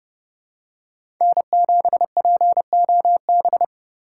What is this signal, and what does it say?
Morse code, heard as a keyed tone, about 30 wpm, 700 Hz. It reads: N7POB
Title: USA Call Signs - 4x Character Spacing - 30wpm